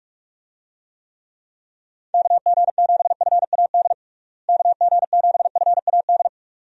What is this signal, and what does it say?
Morse code, heard as a keyed tone, about 42 wpm, 700 Hz. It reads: KG7FAD KG7FAD
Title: USA Call Signs - 6 Characters - Repeated 2x - 45wpm